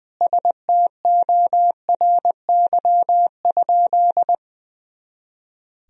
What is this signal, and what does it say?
Morse code, heard as a keyed tone, about 20 wpm, 700 Hz. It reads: STORY?
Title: Sentences from Top 400 Words and Abbreviations 20wpm